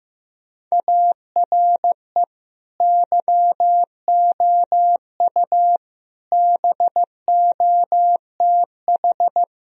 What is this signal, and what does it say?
Morse code, heard as a keyed tone, about 15 wpm, 700 Hz. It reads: ARE YOU BOTH
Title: Sentences from Top 1000 Words - Review Part 1 - 15wpm